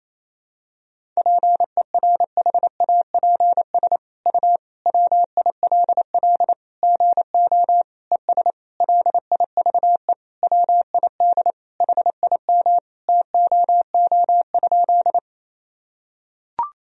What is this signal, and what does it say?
Morse code, heard as a keyed tone, about 28 wpm, 700 Hz. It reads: PERHAPS U WILL GO ES LIVE WID HIM TOO?
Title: Sentences from Top 500 Words and Abbreviations 28wpm